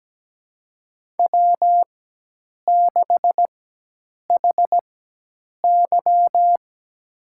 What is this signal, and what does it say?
Morse code, heard as a keyed tone, about 17 wpm, spaced as for 8 wpm, 700 Hz. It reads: W6HY
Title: USA Call Signs - 4 Characters - 4x Character Spacing - 17wpm